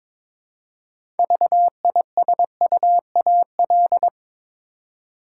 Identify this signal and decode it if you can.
Morse code, heard as a keyed tone, about 22 wpm, 700 Hz. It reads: VISUAL
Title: Six Letter Words - 22wpm